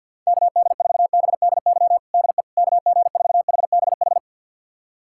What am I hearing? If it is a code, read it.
Morse code, heard as a keyed tone, about 52 wpm, 700 Hz. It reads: KD4BDY DE KG4HBH
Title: USA Callsign DE Callsign 50wpm